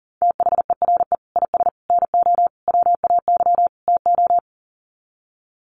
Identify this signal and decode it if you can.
Morse code, heard as a keyed tone, about 40 wpm, 700 Hz. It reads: THERE IS NO WAY TO
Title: Sets of 5 Words Encore 40wpm